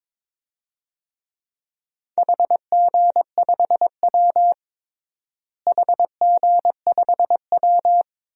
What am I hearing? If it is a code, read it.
Morse code, heard as a keyed tone, about 21 wpm, 700 Hz. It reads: HG5W HG5W
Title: DXCC-100 Call Signs - 4 Characters - Repeated 2x - 22wpm